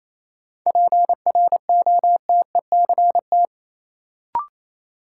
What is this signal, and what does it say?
Morse code, heard as a keyed tone, about 28 wpm, 700 Hz. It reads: PROTECT
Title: Sentences from Top 900 Words - New Word Drill - 28wpm